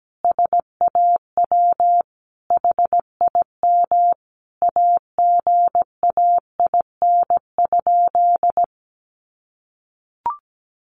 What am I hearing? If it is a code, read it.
Morse code, heard as a keyed tone, about 17 wpm, 700 Hz. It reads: SAW HIM AGAIN?
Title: Sentences from Top 1400 Words 17wpm